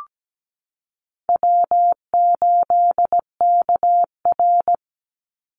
Morse code, heard as a keyed tone, about 17 wpm, 700 Hz. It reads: W8KR
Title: Call Signs - 17wpm